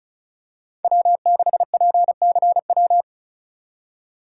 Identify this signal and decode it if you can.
Morse code, heard as a keyed tone, about 34 wpm, 700 Hz. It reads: W6PCW
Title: USA Call Signs - 5 Characters - 2x Character Spacing - 35wpm